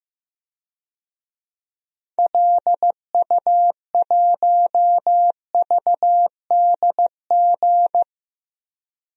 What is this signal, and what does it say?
Morse code, heard as a keyed tone, about 15 wpm, 700 Hz. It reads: LU1VDG
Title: DXCC-100 Call Signs - 6 Characters - Repeated 2x - 15wpm